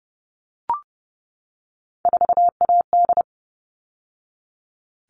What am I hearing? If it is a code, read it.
Morse code, heard as a keyed tone, about 30 wpm, 700 Hz. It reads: 4AD
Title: All Cut Numbers - 3-Digit 30wpm